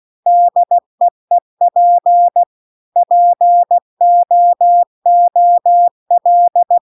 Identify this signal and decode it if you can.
Morse code, heard as a keyed tone, about 16 wpm, 700 Hz. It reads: DEEP POOL
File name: CW-The-Big-Twoheaded-River-2-16WPM